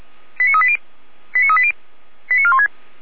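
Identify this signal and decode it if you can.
Five-tone selective call (ZVEI), heard as quick runs of five tones, tones of about 70 ms. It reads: 89380 89380 89416
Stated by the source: ZVEI five-tone selective call 89582, received by SDR at feuerwehr-orlamuende.org (fire brigade Orlamünde).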